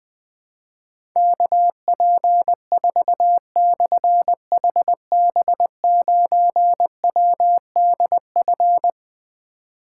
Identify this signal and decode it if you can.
Morse code, heard as a keyed tone, about 20 wpm, 700 Hz. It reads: KP4/HB9WDF
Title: Difficult Call Signs 20wpm